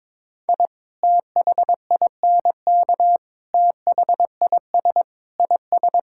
Morse code, heard as a keyed tone, about 22 wpm, 700 Hz. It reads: I THINK THIS IS
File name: practice / 22wpm / Build A Phrase - 5-Word Phrases - 22wpm